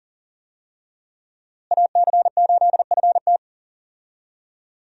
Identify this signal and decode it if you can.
Morse code, heard as a keyed tone, about 40 wpm, 700 Hz. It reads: AC8FT
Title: USA Call Signs - 5 Characters - Repeated 3x - 40wpm